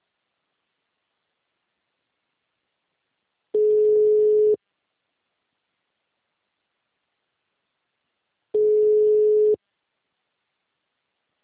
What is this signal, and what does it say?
Ringback tone (Europe, 425 Hz).